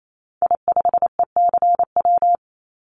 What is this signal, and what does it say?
Morse code, heard as a keyed tone, about 28 wpm, 700 Hz. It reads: I5ECW
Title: Call Signs - 28wpm